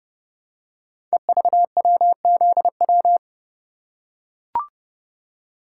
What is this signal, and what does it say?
Morse code, heard as a keyed tone, about 30 wpm, 700 Hz. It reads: EVWZW
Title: Code Groups - 5 Letter:Numbers - 30wpm